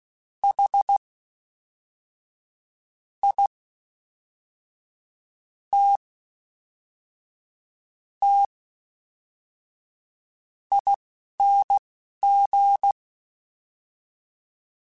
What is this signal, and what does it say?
Morse code, heard as a keyed tone, about 16 wpm, spaced as for 5 wpm, 800 Hz. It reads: HITTING
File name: Special INGs / Special INGs-mp3s / Fast ing @10